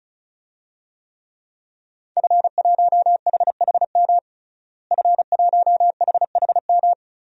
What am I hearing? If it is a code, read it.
Morse code, heard as a keyed tone, about 34 wpm, 700 Hz. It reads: F1HHM F1HHM
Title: DXCC-100 Call Signs - 5 Characters - Repeated 2x - 35wpm